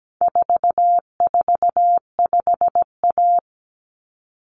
Morse code, heard as a keyed tone, about 17 wpm, 700 Hz. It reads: 445A
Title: Common Cut Numbers - 4-Digit 17wpm